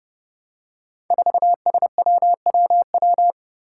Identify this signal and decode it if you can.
Morse code, heard as a keyed tone, about 28 wpm, 700 Hz. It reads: 4SWWW
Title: Five Challenging Letter-Number Combinations 30wpm